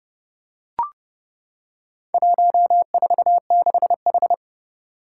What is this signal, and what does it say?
Morse code, heard as a keyed tone, about 31 wpm, 700 Hz. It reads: 146H
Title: Four Challenging Letter-Number Combinations 30wpm